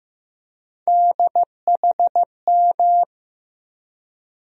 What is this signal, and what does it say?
Morse code, heard as a keyed tone, about 15 wpm, 700 Hz. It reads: DHM
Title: USA Call Sign Postfixes - 3 Letters 15wpm